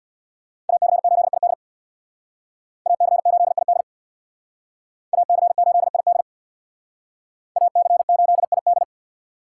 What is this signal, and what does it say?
Morse code, heard as a keyed tone, about 61 wpm, 700 Hz. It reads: AC8ID AC8ID AC8ID AC8ID
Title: Speed Racing - USA 2x2 Call Signs 50wpm